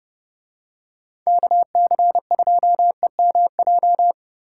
Morse code, heard as a keyed tone, about 32 wpm, 700 Hz. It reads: KC2EMJ